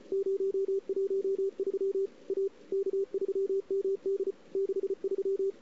Morse code, heard as a keyed tone, about 34 wpm, 410 Hz. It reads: 013 A K3MD 63
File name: CW / 2008-ss-cw-k5tr-op-n5rz -00:00